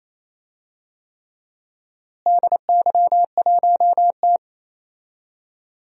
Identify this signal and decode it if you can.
Morse code, heard as a keyed tone, about 28 wpm, 700 Hz. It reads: DY1T
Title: DXCC-100 Call Signs - 4 Characters - Repeated 3x - 28wpm